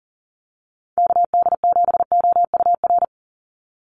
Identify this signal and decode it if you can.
Morse code, heard as a keyed tone, about 40 wpm, 700 Hz. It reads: KD7OUR
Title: USA Call Signs 40wpm